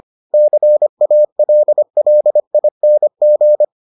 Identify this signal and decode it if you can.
Morse code, heard as a keyed tone, about 25 wpm, 600 Hz. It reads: CALLING